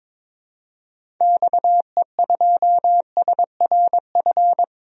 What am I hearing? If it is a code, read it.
Morse code, heard as a keyed tone, about 22 wpm, 700 Hz. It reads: XE2SRF
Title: DXCC-100 Call Signs - 6 Characters - Repeated 3x - 22wpm